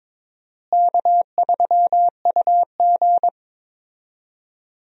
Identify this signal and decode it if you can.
Morse code, heard as a keyed tone, about 21 wpm, 700 Hz. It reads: K3UG
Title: USA Call Signs - 4 Characters - Repeated 3x - 22wpm